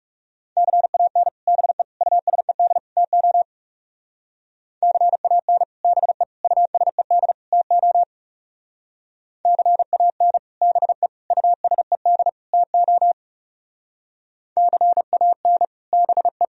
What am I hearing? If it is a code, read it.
Morse code, heard as a keyed tone, about 35 wpm, 700 Hz. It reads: CAN BE USED TO CAN BE USED TO CAN BE USED TO CAN BE
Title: Speed-Racing - Sets of 4 Words 30wpm